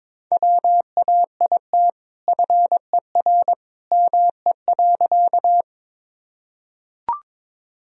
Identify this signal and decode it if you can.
Morse code, heard as a keyed tone, about 22 wpm, 700 Hz. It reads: WAIT FER ME.
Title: Sentences from Top 500 Words and Abbreviations 22wpm